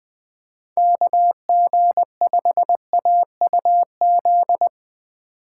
Morse code, heard as a keyed tone, about 21 wpm, 700 Hz. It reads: KG5AUZ